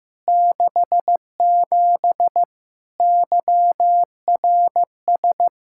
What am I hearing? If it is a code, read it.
Morse code, heard as a keyed tone, about 15 wpm, 700 Hz. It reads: 67 YRS
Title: Practice QSO - Exchange 3 - More Info - 15wpm